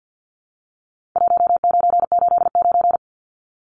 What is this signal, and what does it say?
Morse code, heard as a keyed tone, about 50 wpm, 700 Hz. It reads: 1989
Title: Numbers - 4-Digit 50wpm